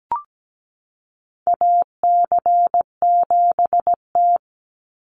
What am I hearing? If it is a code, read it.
Morse code, heard as a keyed tone, about 17 wpm, 700 Hz. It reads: AC7T
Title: Call Signs Repeated 2x - 17wpm